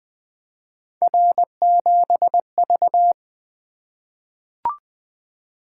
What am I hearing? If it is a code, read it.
Morse code, heard as a keyed tone, about 20 wpm, 700 Hz. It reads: R7V
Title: Three Letter-Number Combinations - 20wpm